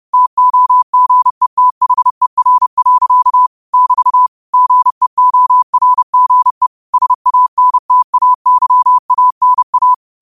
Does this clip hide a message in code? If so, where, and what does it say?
Yes: Morse code from 0.1 to 9.9 s, heard as a keyed tone, about 30 wpm, 1000 Hz. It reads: TOGETHER. <BT> GEORGE SANTAYANA